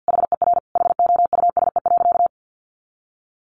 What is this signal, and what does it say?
Morse code, heard as a keyed tone, about 50 wpm, 700 Hz. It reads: HER HOUSE.
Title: Sentences from Top 1500 Words 50wpm